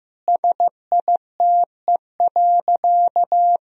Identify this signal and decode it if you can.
Morse code, heard as a keyed tone, about 15 wpm, 700 Hz. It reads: SITE.
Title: Sentences from Top 1000 Words - Review Part 4 - 15wpm